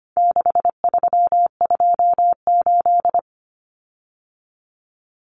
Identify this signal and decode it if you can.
Morse code, heard as a keyed tone, about 25 wpm, 700 Hz. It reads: 6328
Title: Common Cut Numbers - 4-Digit 25wpm